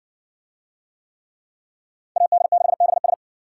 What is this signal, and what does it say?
Morse code, heard as a keyed tone, about 56 wpm, 700 Hz. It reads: AD6BS